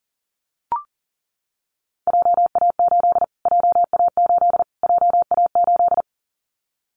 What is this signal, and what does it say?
Morse code, heard as a keyed tone, about 40 wpm, 700 Hz. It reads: JA8 JA8 JA8
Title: Callsign Prefixes for Top 25 DXCC - Repeated 3x - 40wpm